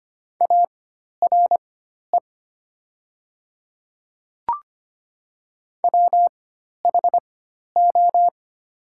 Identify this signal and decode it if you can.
Morse code, heard as a keyed tone, about 25 wpm, spaced as for 12 wpm, 700 Hz. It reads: ARE WHO
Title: 4x Character Spacing - Top 100 Words - 25wpm